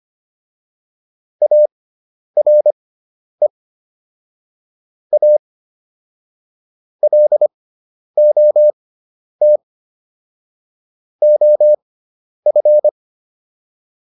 Morse code, heard as a keyed tone, about 25 wpm, spaced as for 10 wpm, 600 Hz. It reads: ARE A LOT OF